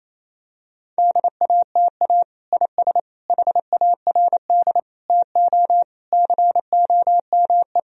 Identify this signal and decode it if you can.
Morse code, heard as a keyed tone, about 28 wpm, 700 Hz. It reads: DATA IS HARD TO COME
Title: Sentences from Top 1000 Words - Review Part 3 - 28wpm